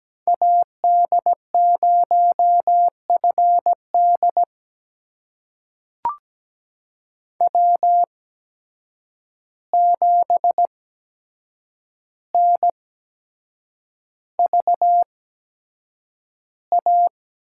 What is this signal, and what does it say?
Morse code, heard as a keyed tone, about 17 wpm, spaced as for 7 wpm, 700 Hz. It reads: AD0FD W7NVA
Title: USA Call Signs - 8x Character Spacing - 17wpm